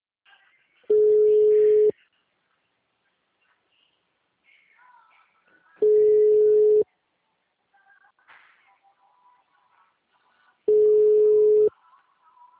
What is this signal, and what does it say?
Ringback tone (Europe, 425 Hz).